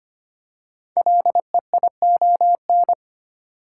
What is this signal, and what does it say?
Morse code, heard as a keyed tone, about 25 wpm, 700 Hz. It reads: LEION
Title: Code Groups - 5 Letter:Numbers - 25wpm